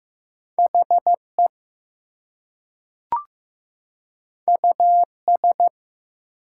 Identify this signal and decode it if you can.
Morse code, heard as a keyed tone, about 15 wpm, 700 Hz. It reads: HE US
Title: Two Letter Words - 15wpm